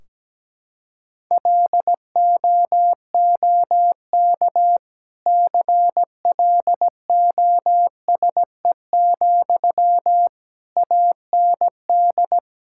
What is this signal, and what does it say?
Morse code, heard as a keyed tone, about 17 wpm, 700 Hz. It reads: LOOK CLOSE, AND
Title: Sentences from Top 1600 Words 17wpm